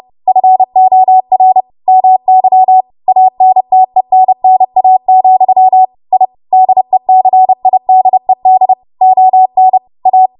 Morse code, heard as a keyed tone, about 30 wpm, 750 Hz. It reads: FOR MY ANTENNA, I DECIDED ON A
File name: Morse / 30WPM